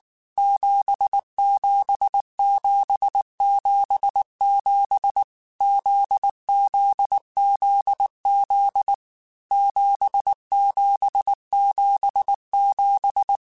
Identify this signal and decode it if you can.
Morse code, heard as a keyed tone, about 19 wpm, 800 Hz. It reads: 77777 ZZZZ 7777
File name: Basic Sets / Basic Letter Groups Collective7 / JZ17@18